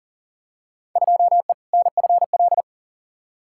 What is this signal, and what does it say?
Morse code, heard as a keyed tone, about 40 wpm, 700 Hz. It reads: 2E NFL
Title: Field Day Exchanges Encore - 40wpm